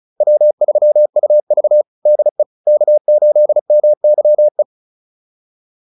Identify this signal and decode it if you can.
Morse code, heard as a keyed tone, about 34 wpm, 600 Hz. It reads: W3UV DE K8MYE
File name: QSO403 35